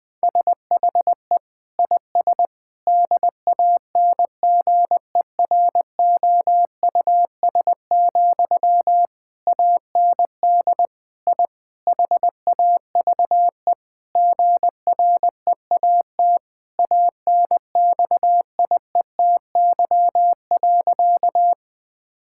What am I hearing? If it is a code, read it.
Morse code, heard as a keyed tone, about 20 wpm, 700 Hz. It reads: SHE IS DANGEROUS, AND I HAVE GREAT ANXIETY.
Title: Sentences from Top 1600 Words 20wpm